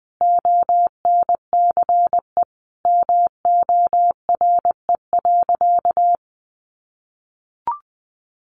Morse code, heard as a keyed tone, about 20 wpm, 700 Hz. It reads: ONCE MORE.
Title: Sentences from Top 1100 Words 20wpm